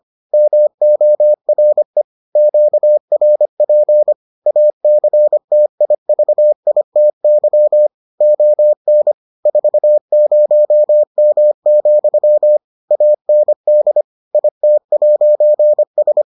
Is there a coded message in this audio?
Yes: Morse code from 0.3 to 16.2 s, heard as a keyed tone, about 25 wpm, 600 Hz. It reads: MORE QRP ACTIVITY ON 40M, AND IT'S